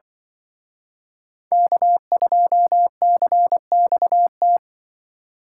Morse code, heard as a keyed tone, about 24 wpm, 700 Hz. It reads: K2CXT
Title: Speed Racing - USA 1x3 Call Signs 20wpm